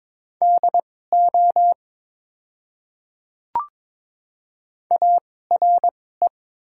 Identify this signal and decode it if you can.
Morse code, heard as a keyed tone, about 22 wpm, spaced as for 16 wpm, 700 Hz. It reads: DO ARE
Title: 2x Character Spacing - Top 100 Words - 22wpm